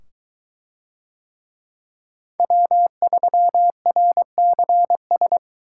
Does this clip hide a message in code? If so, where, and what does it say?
Yes: Morse code from 2.4 to 5.4 s, heard as a keyed tone, about 21 wpm, 700 Hz. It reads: W3RCS